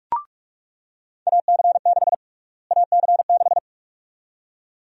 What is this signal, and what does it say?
Morse code, heard as a keyed tone, about 42 wpm, 700 Hz. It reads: AC6 AC6
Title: USA Call Sign Prefixes - Repeated 2x - 45wpm